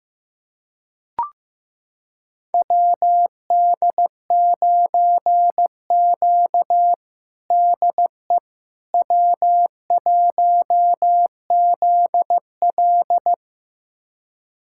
Morse code, heard as a keyed tone, about 15 wpm, 700 Hz. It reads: WD9Q DE W1ZL
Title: USA Callsign DE Callsign - 4 Characters - 15wpm